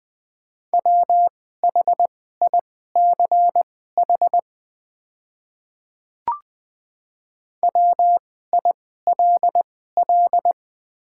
Morse code, heard as a keyed tone, about 20 wpm, spaced as for 15 wpm, 700 Hz. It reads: WHICH WILL
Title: 2x Character Spacing - Top 100 Words - 20wpm